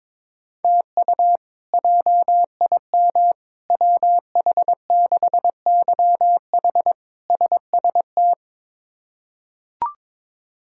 Morse code, heard as a keyed tone, about 22 wpm, 700 Hz. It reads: TU JIM WH6YH SST
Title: K1USN SST QSO Practice - 22wpm